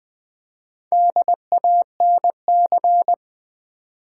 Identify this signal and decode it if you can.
Morse code, heard as a keyed tone, about 20 wpm, 690 Hz. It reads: DANC
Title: Build A Word - 2 Letters At A Time - 6-Letter Words - 20wpm